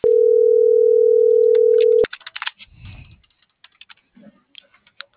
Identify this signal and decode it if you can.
Ringback tone (North America), 440 + 480 Hz.